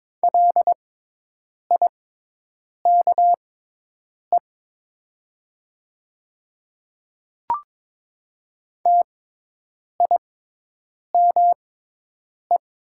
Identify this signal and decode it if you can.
Morse code, heard as a keyed tone, about 22 wpm, spaced as for 8 wpm, 700 Hz. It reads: LIKE TIME
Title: 6x Character Spacing - Top 100 Words - 22wpm